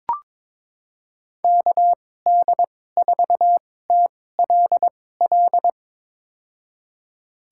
Morse code, heard as a keyed tone, about 21 wpm, spaced as for 16 wpm, 700 Hz. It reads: KD4TLL
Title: USA Call Signs - 6 Characters - 2x Character Spacing - 22wpm